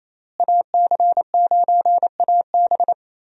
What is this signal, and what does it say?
Morse code, heard as a keyed tone, about 27 wpm, 700 Hz. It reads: AC9AB
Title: USA Call Signs - 5 Characters - Repeated 2x - 28wpm